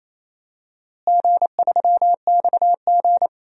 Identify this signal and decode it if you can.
Morse code, heard as a keyed tone, about 27 wpm, 700 Hz. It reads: G3XG
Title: DXCC-100 Call Signs - 4 Characters - Repeated 3x - 28wpm